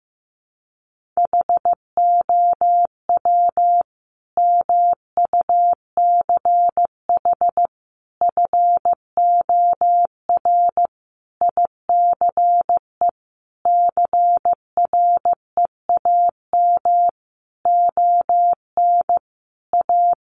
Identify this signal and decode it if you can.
Morse code, heard as a keyed tone, about 15 wpm, 700 Hz. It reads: HOW MUCH FOR ICE CREAM ON A
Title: Sentences from Top 1500 Words 15wpm